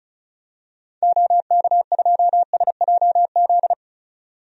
Morse code, heard as a keyed tone, about 35 wpm, 700 Hz. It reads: OK2SJZ
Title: DXCC-100 Call Signs - 6 Characters - 35wpm